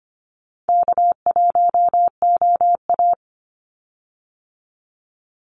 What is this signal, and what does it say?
Morse code, heard as a keyed tone, about 25 wpm, 700 Hz. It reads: K1OA
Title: Easy Call Signs Repeated 3x - 25wpm